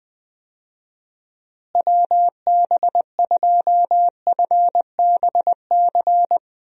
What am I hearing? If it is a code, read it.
Morse code, heard as a keyed tone, about 20 wpm, 700 Hz. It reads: WB2FBC
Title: USA Call Signs - 6 Characters - Repeated 3x - 20wpm